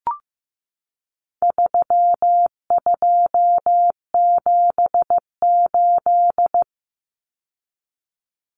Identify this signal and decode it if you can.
Morse code, heard as a keyed tone, about 15 wpm, 700 Hz. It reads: 3278
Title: Common Cut Numbers - 4-Digit 15wpm